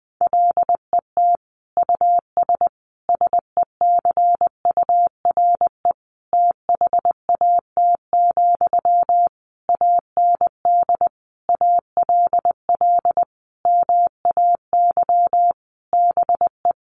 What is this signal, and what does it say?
Morse code, heard as a keyed tone, about 20 wpm, 700 Hz. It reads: LET US SECURE THAT, AND ALL MAY BE
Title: Sentences from Top 1200 Words 20wpm